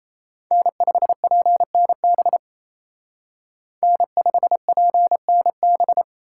Morse code, heard as a keyed tone, about 35 wpm, 700 Hz. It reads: N5PNB N5PNB
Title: Speed Racing - USA 1x3 Call Signs 28wpm